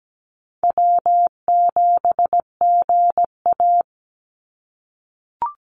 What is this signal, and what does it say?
Morse code, heard as a keyed tone, about 17 wpm, 700 Hz. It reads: W7GA
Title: NA SOTA Chaser Callsigns 17wpm